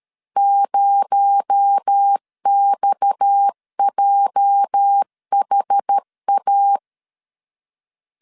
Morse code, heard as a keyed tone, about 13 wpm, 800 Hz. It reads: 0XJHA
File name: LEZIONI / 8 YJ45 / 8 YJ45 12wpm 2151